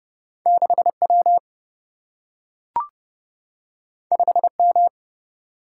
Morse code, heard as a keyed tone, about 30 wpm, 700 Hz. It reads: 6 W 5 M
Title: Single-Letter Single-Number 30wpm